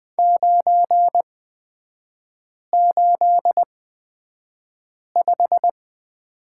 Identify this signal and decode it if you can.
Morse code, heard as a keyed tone, about 20 wpm, spaced as for 5 wpm, 700 Hz. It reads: 985
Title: 243 Lesson 39e - Letter z - Three Characters 20wpm